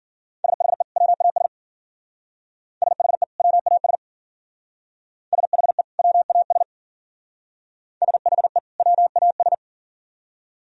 Morse code, heard as a keyed tone, about 51 wpm, 700 Hz. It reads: SHE WAS SHE WAS SHE WAS SHE WAS